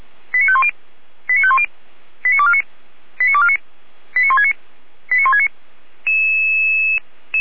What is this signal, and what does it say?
Five-tone selective call (ZVEI), heard as quick runs of five tones, tones of about 70 ms. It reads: 89510 89510 89249 89249 89169 89169
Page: /455/SDR/ZVEI/